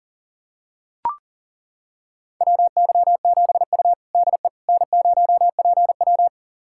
Morse code, heard as a keyed tone, about 40 wpm, 700 Hz. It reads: WY7U DE N0PW
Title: USA Callsign DE Callsign - 4 Characters - 40wpm